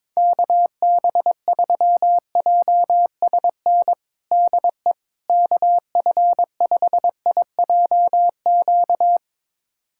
Morse code, heard as a keyed tone, about 22 wpm, 700 Hz. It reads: KB3JSN DE KF5IJQ
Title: USA Callsign DE Callsign - 6 Characters - 22wpm